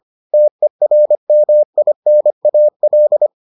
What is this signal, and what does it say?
Morse code, heard as a keyed tone, about 25 wpm, 600 Hz. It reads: TERMINAL